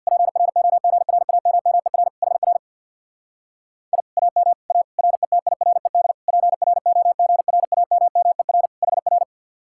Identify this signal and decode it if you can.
Morse code, heard as a keyed tone, about 64 wpm, 700 Hz. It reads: PROGRAMMER HR I AM A RETIRED PROGRAMMER HR